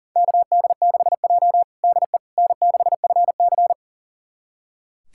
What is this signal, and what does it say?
Morse code, heard as a keyed tone, about 37 wpm, 690 Hz. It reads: KD6J DE N6FC
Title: USA Callsign DE Callsign - 4 Characters - 40wpm